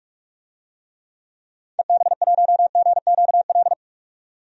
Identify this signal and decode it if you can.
Morse code, heard as a keyed tone, about 44 wpm, 690 Hz. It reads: EB1GQL
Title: DXCC-100 Call Signs - 6 Characters - 45wpm